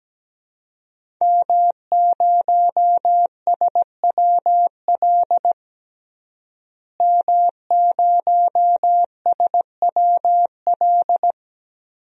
Morse code, heard as a keyed tone, about 17 wpm, 700 Hz. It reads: M0SWL M0SWL
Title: DXCC-100 Call Signs - 5 Characters - Repeated 3x - 17wpm